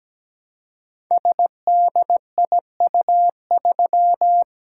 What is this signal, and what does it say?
Morse code, heard as a keyed tone, about 17 wpm, 700 Hz. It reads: SDIU3
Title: Code Groups - 5 Letter:Numbers - 17wpm